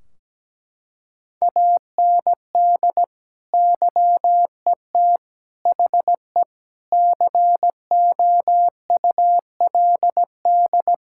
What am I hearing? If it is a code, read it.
Morse code, heard as a keyed tone, about 17 wpm, 700 Hz. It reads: AND YET HE COULD
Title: Sentences from Top 1600 Words 17wpm